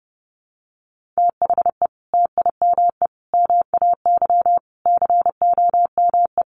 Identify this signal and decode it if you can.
Morse code, heard as a keyed tone, about 30 wpm, 700 Hz. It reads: THE TIME MAY COME
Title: Sentences from Top 1100 Words 30wpm